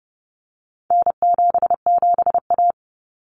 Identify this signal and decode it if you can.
Morse code, heard as a keyed tone, about 30 wpm, 700 Hz. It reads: N77A
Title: Common Cut Numbers - 4-Digit 30wpm